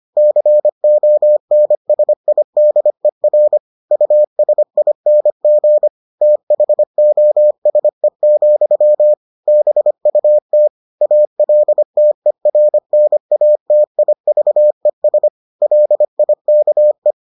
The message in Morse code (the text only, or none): CONSIDER USING THOSE, BUT ALTERNATIVES LIKE